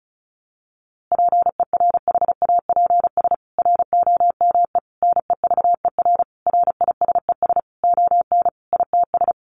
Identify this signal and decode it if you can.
Morse code, heard as a keyed tone, about 34 wpm, 700 Hz. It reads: PERHAPS ROME NEVER RISES ON ITS
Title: A Sound Of Thunder 35wpm